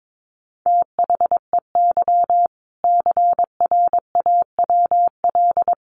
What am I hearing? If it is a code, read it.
Morse code, heard as a keyed tone, about 22 wpm, 700 Hz. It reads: THEY CRAWL
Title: Additional Easy Phrases 22wpm